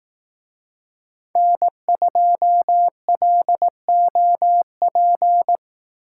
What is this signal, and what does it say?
Morse code, heard as a keyed tone, about 19 wpm, 700 Hz. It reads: N2LOP
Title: Speed Racing - USA Call Signs 15wpm